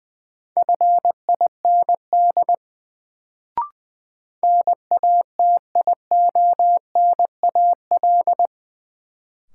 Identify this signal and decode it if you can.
Morse code, heard as a keyed tone, about 20 wpm, 700 Hz. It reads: FIND NATIONAL
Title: Sets of 2 Words - New Word Drill - 20wpm